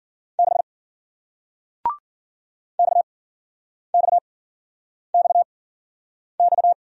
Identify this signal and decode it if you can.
Morse code, heard as a keyed tone, about 56 wpm, 700 Hz. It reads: 6 X X X X